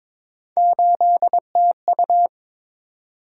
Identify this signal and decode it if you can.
Morse code, heard as a keyed tone, about 22 wpm, 700 Hz. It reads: 8TU
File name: practice / 22wpm / Three Letter-Number Combinations - 22wpm